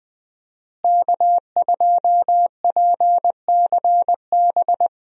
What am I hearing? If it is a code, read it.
Morse code, heard as a keyed tone, about 21 wpm, 700 Hz. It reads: K2PCB
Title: USA Call Signs - 5 Characters - Repeated 2x - 20wpm